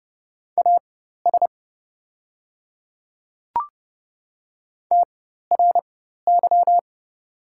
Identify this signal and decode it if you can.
Morse code, heard as a keyed tone, about 30 wpm, spaced as for 14 wpm, 700 Hz. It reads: AS TRY
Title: 4x Character Spacing - Top 200 Words - 30wpm